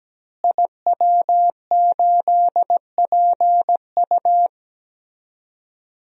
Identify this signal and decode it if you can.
Morse code, heard as a keyed tone, about 17 wpm, 700 Hz. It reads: IW8PU
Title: DXCC-100 Call Signs - 5 Characters - Repeated 2x - 17wpm